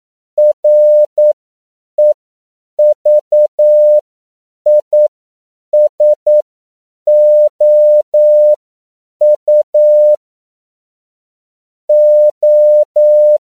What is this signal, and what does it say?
Morse code, heard as a keyed tone, about 9 wpm, spaced as for 7 wpm, 600 Hz. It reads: REVISOU O
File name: CW / 10PPM